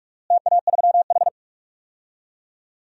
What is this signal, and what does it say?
Morse code, heard as a keyed tone, about 46 wpm, 700 Hz. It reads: TA3H
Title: DXCC-100 Call Signs - 4 Characters - Repeated 2x - 45wpm